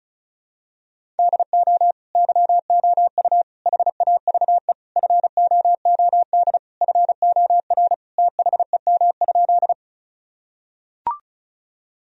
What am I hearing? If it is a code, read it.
Morse code, heard as a keyed tone, about 34 wpm, 700 Hz. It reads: DO YOU HAVE FOOD FOR THEM?
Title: Sentences from Top 1000 Words - Review Part 1 - 35wpm